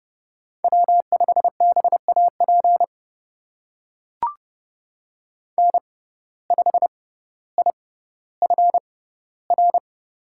Morse code, heard as a keyed tone, about 30 wpm, spaced as for 15 wpm, 700 Hz. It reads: W5BAP N5IFR
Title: USA Call Signs - 5 Characters - 6x Character Spacing - 30wpm